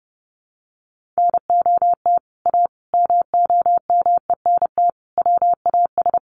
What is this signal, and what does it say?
Morse code, heard as a keyed tone, about 30 wpm, 700 Hz. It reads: NOT A MOMENT WAS
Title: Sentences from Top 1400 Words 30wpm